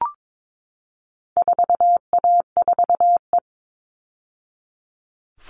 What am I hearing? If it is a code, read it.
Morse code, heard as a keyed tone, about 22 wpm, 700 Hz. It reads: 4A4E
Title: All Cut Numbers - 4-Digit 22wpm